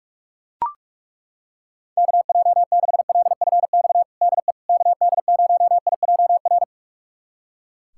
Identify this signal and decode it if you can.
Morse code, heard as a keyed tone, about 42 wpm, 700 Hz. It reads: KJ6LFX DE KD0IJR
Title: USA Callsign DE Callsign - 6 Characters - 45wpm